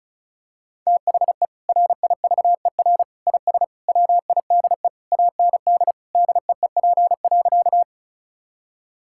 Morse code, heard as a keyed tone, about 36 wpm, 700 Hz. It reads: THE RIVER IS WIDE AND DEEP.
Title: Sentences from Top 1000 Words - Review Part 6 - 35wpm